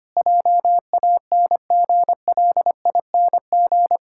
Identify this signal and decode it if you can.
Morse code, heard as a keyed tone, about 25 wpm, 700 Hz. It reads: JANGLING
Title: Pangram Challenge v2 - 25wpm